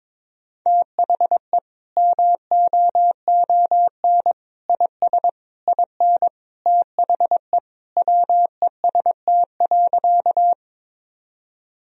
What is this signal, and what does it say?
Morse code, heard as a keyed tone, about 22 wpm, 700 Hz. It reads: THE MOON IS IN THE WEST.
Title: Sentences from Top 1000 Words - Review Part 3 - 22wpm